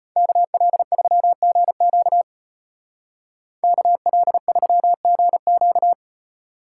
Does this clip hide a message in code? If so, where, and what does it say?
Yes: Morse code from 0.2 to 5.9 s, heard as a keyed tone, about 34 wpm, 700 Hz. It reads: KL3GQ KL3GQ